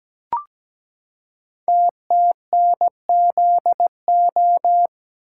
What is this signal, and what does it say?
Morse code, heard as a keyed tone, about 17 wpm, 700 Hz. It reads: TTNZO
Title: Code Groups - 5 Letters - 17wpm